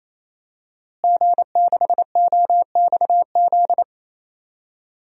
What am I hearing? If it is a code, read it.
Morse code, heard as a keyed tone, about 28 wpm, 700 Hz. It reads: G6OXZ
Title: DXCC-100 Call Signs - 5 Characters - Repeated 3x - 28wpm